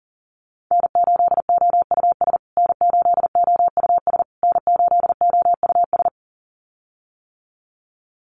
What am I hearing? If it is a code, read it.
Morse code, heard as a keyed tone, about 40 wpm, 700 Hz. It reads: N8OUS N8OUS N8OUS
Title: USA Call Signs Repeated 3x 40wpm